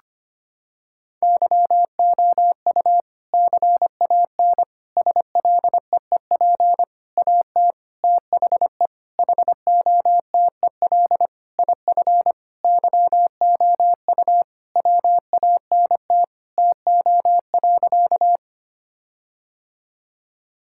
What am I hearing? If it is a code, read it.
Morse code, heard as a keyed tone, about 25 wpm, 700 Hz. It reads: YOU CAN SLEEP AT THE HOTEL IF YOU WANT TO.
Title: Sentences from Top 1600 Words 25wpm